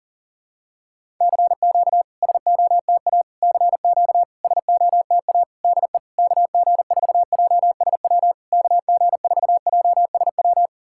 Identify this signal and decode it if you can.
Morse code, heard as a keyed tone, about 40 wpm, 700 Hz. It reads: CQ SOTA CQ SOTA DE KG4JSW KG4JSW
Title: SOTA QSO Practice 40wpm